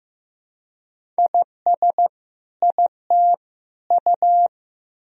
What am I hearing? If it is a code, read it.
Morse code, heard as a keyed tone, about 15 wpm, 700 Hz. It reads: IS IT U
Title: Sentences from Top 200 Words and Abbreviations 15wpm